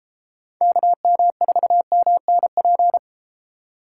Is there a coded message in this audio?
Yes: Morse code from 0.6 to 3.0 s, heard as a keyed tone, about 32 wpm, 700 Hz. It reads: KM4MNP